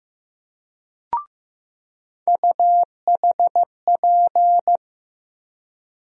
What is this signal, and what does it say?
Morse code, heard as a keyed tone, about 15 wpm, 700 Hz. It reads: UHP
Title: USA Call Sign Postfixes - 3 Letters 15wpm